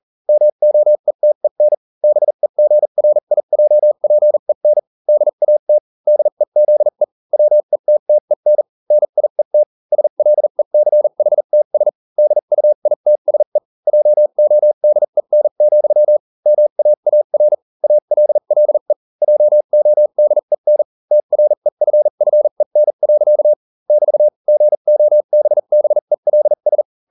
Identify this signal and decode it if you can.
Morse code, heard as a keyed tone, about 40 wpm, spaced as for 35 wpm, 600 Hz. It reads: MOETEN BEGRIJPEN DAT DEZE WETTEN NIET SLECHTS DUITSE JODEN, MAAR ALLE JODEN TREFFEN. <BT> GOBBELS